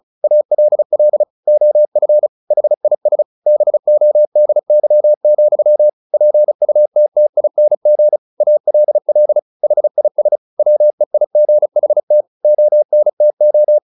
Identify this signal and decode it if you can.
Morse code, heard as a keyed tone, about 34 wpm, 600 Hz. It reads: ALL OF HIS BODY, PUTTING ALL HIS WEIGHT ONTO
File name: the old man and the sea 2 35wpm